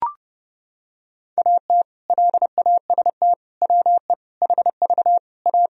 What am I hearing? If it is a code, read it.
Morse code, heard as a keyed tone, about 32 wpm, 700 Hz. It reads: AT LAST WE HV A
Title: Sentences from Top 300 Words and Abbreviations 30wpm